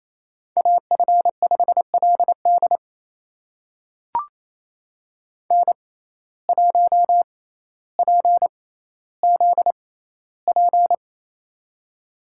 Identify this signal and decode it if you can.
Morse code, heard as a keyed tone, about 28 wpm, spaced as for 14 wpm, 700 Hz. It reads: AF5LD N1PZP
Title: USA Call Signs - 5 Characters - 6x Character Spacing - 28wpm